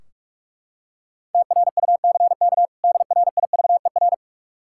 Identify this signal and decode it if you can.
Morse code, heard as a keyed tone, about 42 wpm, 700 Hz. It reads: TRUCK DRIVER
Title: Practice QSO - Exchange 3 - More Info - 45wpm